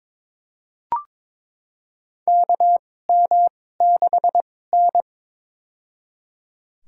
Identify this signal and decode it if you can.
Morse code, heard as a keyed tone, about 22 wpm, spaced as for 16 wpm, 700 Hz. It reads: KM6N
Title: USA Call Signs - 4 Characters - 2x Character Spacing - 22wpm